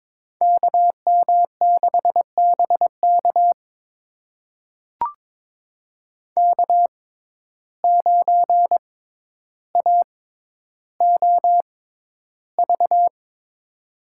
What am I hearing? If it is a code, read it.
Morse code, heard as a keyed tone, about 22 wpm, spaced as for 11 wpm, 700 Hz. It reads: KM6BK K9AOV
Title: USA Call Signs - 5 Characters - 6x Character Spacing - 22wpm